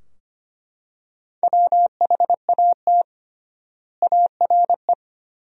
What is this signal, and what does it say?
Morse code, heard as a keyed tone, about 25 wpm, 700 Hz. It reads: WHAT ARE
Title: 3x Word-Spacing - Sentences from Top 100 Words - 25wpm